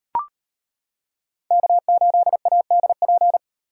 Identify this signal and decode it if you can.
Morse code, heard as a keyed tone, about 31 wpm, 700 Hz. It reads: K8ADP
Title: Speed Racing - USA 1x3 Call Signs 25wpm